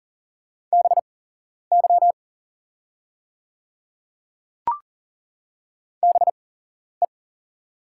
Morse code, heard as a keyed tone, about 40 wpm, spaced as for 11 wpm, 700 Hz. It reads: BY BE
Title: 8x Character Spacing - Top 100 Words - 40wpm